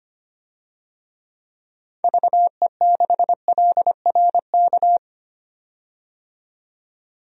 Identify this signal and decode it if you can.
Morse code, heard as a keyed tone, about 25 wpm, 700 Hz. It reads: VE6LRK
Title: DXCC-100 Call Signs - 6 Characters - 25wpm